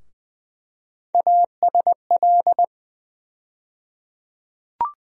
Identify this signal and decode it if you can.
Morse code, heard as a keyed tone, about 20 wpm, 700 Hz. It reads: ASL
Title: USA Call Sign Postfixes - 3 Letters 20wpm